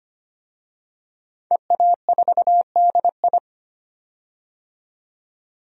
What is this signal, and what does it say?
Morse code, heard as a keyed tone, about 25 wpm, 700 Hz. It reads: EA4DI
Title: DXCC-100 Call Signs - 5 Characters - 25wpm